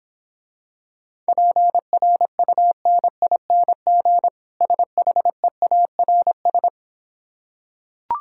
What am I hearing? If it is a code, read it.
Morse code, heard as a keyed tone, about 26 wpm, 700 Hz. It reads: PRUNING SHEARS